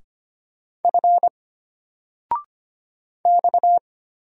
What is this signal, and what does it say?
Morse code, heard as a keyed tone, about 25 wpm, 700 Hz. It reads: F X
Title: Single Letter-Number 25wpm